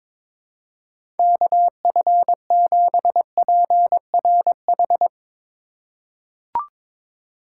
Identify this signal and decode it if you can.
Morse code, heard as a keyed tone, about 22 wpm, 700 Hz. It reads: KF7PRH
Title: USA Call Signs - 4x Character Spacing - 22wpm